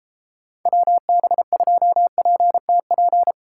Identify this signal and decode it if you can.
Morse code, heard as a keyed tone, about 29 wpm, 700 Hz. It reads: WB2PTP